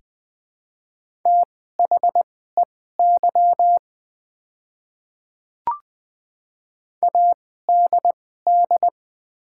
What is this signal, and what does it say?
Morse code, heard as a keyed tone, about 20 wpm, spaced as for 14 wpm, 700 Hz. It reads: THEY ADD
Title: 2x Character Spacing - Top 300 Words - 20wpm